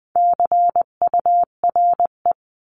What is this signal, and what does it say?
Morse code, heard as a keyed tone, about 20 wpm, 700 Hz. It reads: CURE
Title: 259 Lesson 42d - Words 18-20wpm